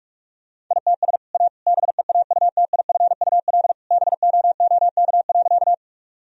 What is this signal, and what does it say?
Morse code, heard as a keyed tone, about 41 wpm, 700 Hz. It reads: ITS A BEAUTIFUL BOOK.
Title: Sentences from Top 1000 Words - Review Part 1 - 45wpm